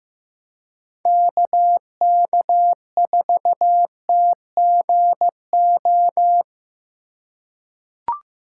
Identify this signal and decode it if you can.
Morse code, heard as a keyed tone, about 15 wpm, 700 Hz. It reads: KK4TGO
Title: USA Call Signs - 6 Characters - 4x Character Spacing - 15wpm